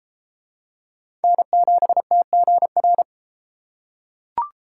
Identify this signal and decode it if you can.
Morse code, heard as a keyed tone, about 31 wpm, 700 Hz. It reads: N7TGR